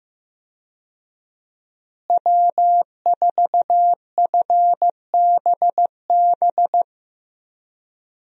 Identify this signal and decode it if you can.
Morse code, heard as a keyed tone, about 16 wpm, 700 Hz. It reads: W4FBB
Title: USA Call Signs - 5 Characters - Repeated 2x - 15wpm